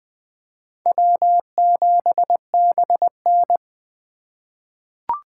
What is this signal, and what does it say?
Morse code, heard as a keyed tone, about 20 wpm, 700 Hz. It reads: W7BN
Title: USA Call Signs - 4 Characters - 2x Character Spacing - 20wpm